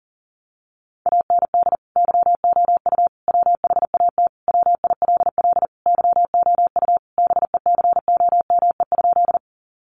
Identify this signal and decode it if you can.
Morse code, heard as a keyed tone, about 40 wpm, 700 Hz. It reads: AND YOU WHAT WILL YOU BECOME?
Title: Sentences from Top 600 Words 40wpm